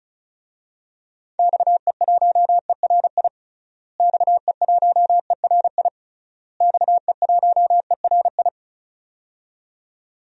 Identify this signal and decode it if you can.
Morse code, heard as a keyed tone, about 36 wpm, 700 Hz. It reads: XE1ERI XE1ERI XE1ERI
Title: DXCC-100 Call Signs - 6 Characters - Repeated 3x - 35wpm